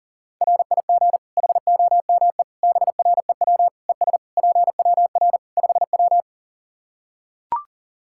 Morse code, heard as a keyed tone, about 40 wpm, 700 Hz. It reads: RIG HOME BREW ES PWR 5W
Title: K1USN SST QSO Practice 40wpm